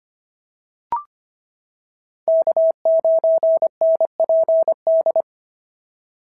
Morse code, heard as a keyed tone, about 25 wpm, 690 Hz. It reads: K9NPD
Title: SOTA QSO Practice 25wpm